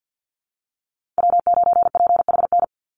Morse code, heard as a keyed tone, about 50 wpm, 700 Hz. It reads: W9PHN
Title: USA Call Signs 50wpm